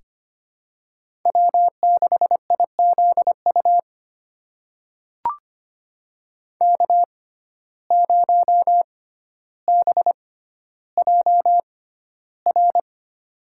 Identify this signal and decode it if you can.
Morse code, heard as a keyed tone, about 25 wpm, spaced as for 13 wpm, 700 Hz. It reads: W6IZU K0BJR
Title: USA Call Signs - 6x Character Spacing - 25wpm